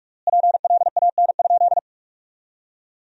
Morse code, heard as a keyed tone, about 46 wpm, 700 Hz. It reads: PLAN?
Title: 3x Word-Spacing - Sentences from Top 500 Words - 45wpm